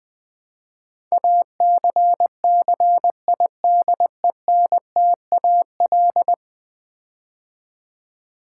Morse code, heard as a keyed tone, about 20 wpm, 700 Hz. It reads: ACCIDENTAL
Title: Build A Word - 5 Letters At A Time - 12-Letter Words - 20wpm